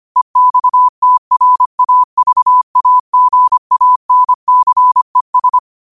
Morse code, heard as a keyed tone, about 25 wpm, 1000 Hz. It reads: EXTRAVAGANCES
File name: qotdfiles / 25wpm